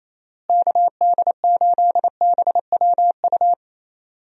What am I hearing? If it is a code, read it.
Morse code, heard as a keyed tone, about 27 wpm, 700 Hz. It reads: KD8BWU
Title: USA Call Signs - 6 Characters - Repeated 2x - 28wpm